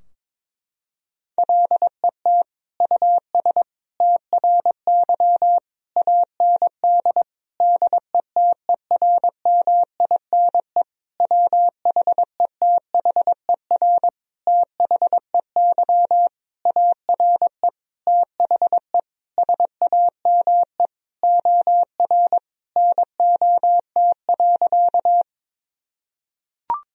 Morse code, heard as a keyed tone, about 22 wpm, 700 Hz. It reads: LET US TRY AND DETERMINE WHETHER THEY ARE THE SAME OR NOT.
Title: Sentences from Top 1000 Words - Review Part 4 - 22wpm